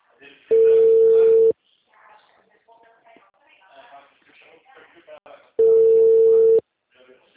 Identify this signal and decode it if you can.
Ringback tone (China, 450 Hz), with speech.
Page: TLML PRUDENTE